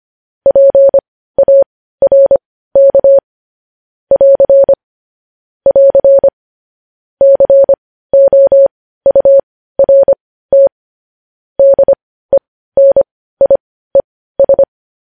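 Morse code, heard as a keyed tone, about 25 wpm, spaced as for 15 wpm, 550 Hz. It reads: PARK <AR> <AR> COURT DENIES
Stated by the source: Morse code news headlines sent at 15, preceded by vvv vvv.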